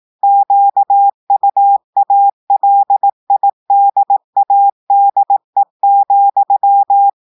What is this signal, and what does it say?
Morse code, heard as a keyed tone, about 18 wpm, 800 Hz. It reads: QUALIDADE,
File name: portugais texte en clair 2 vitesse 18 mots minutes